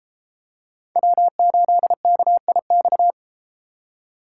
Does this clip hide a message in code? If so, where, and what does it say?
Yes: Morse code from 1.0 to 3.1 s, heard as a keyed tone, about 31 wpm, 700 Hz. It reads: W8KIX